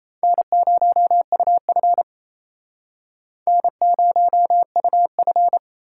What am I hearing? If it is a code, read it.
Morse code, heard as a keyed tone, about 38 wpm, 700 Hz. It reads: N0UF N0UF